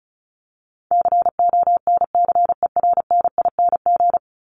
Morse code, heard as a keyed tone, about 34 wpm, 700 Hz. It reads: CONCERNING
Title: ING Suffix - Top 100 Words with Base Words 35wpm